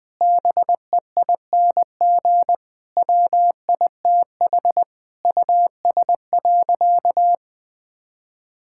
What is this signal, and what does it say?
Morse code, heard as a keyed tone, about 21 wpm, 700 Hz. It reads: BEING WITH US.
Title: Sentences with 5-Word Phrases - 20wpm